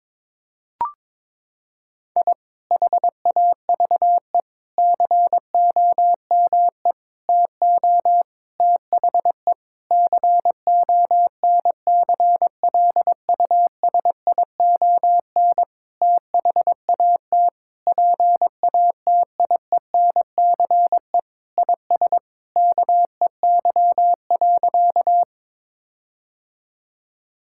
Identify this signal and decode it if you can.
Morse code, heard as a keyed tone, about 22 wpm, 700 Hz. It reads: I HAVE COME TO THE CONCLUSION THAT PATIENCE IS KEY.
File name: Sentences with 7-Word Phrases - 22wpm